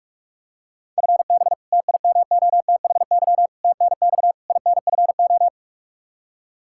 Morse code, heard as a keyed tone, about 46 wpm, 700 Hz. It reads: FB TIMOTHY TNX INFO
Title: K1USN SST QSO Practice 45wpm